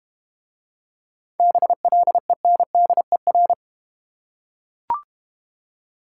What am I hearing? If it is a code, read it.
Morse code, heard as a keyed tone, about 32 wpm, 700 Hz. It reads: BLENDER